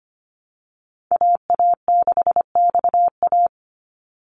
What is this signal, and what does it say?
Morse code, heard as a keyed tone, about 25 wpm, 700 Hz. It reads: AA6XA
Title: Call Signs Repeated 3x - 25wpm